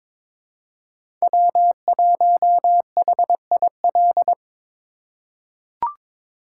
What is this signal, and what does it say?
Morse code, heard as a keyed tone, about 22 wpm, 700 Hz. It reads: W1HIL
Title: USA Call Signs - 5 Characters - 2x Character Spacing - 22wpm